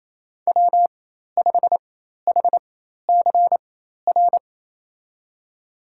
Morse code, heard as a keyed tone, about 27 wpm, spaced as for 13 wpm, 700 Hz. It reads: W5HCR
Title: USA Call Signs - 5 Characters - 4x Character Spacing - 28wpm